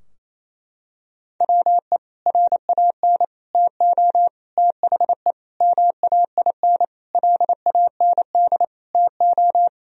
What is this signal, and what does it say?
Morse code, heard as a keyed tone, about 28 wpm, 690 Hz. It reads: WE RAN TO THE MAIN LAND TO
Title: Sentences from Top 1000 Words - Review Part 2 - 28wpm